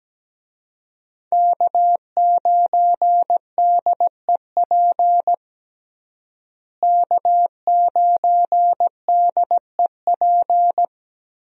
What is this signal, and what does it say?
Morse code, heard as a keyed tone, about 17 wpm, 700 Hz. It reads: K9DEP K9DEP
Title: USA Call Signs - 5 Characters - Repeated 3x - 17wpm